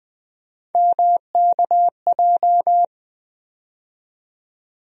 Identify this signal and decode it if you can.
Morse code, heard as a keyed tone, about 20 wpm, 700 Hz. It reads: MKJ
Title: USA Call Sign Postfixes - 3 Letters 20wpm